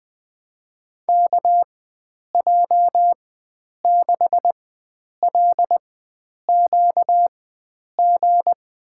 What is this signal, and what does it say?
Morse code, heard as a keyed tone, about 21 wpm, spaced as for 9 wpm, 700 Hz. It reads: KJ6LQG
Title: USA Call Signs - 6 Characters - 4x Character Spacing - 20wpm